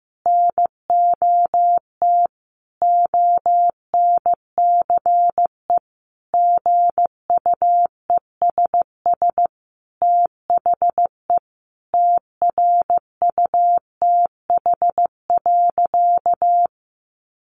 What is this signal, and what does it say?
Morse code, heard as a keyed tone, about 15 wpm, 700 Hz. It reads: NOT ONCE GUESS THE TRUTH.
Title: Sentences from Top 1500 Words 15wpm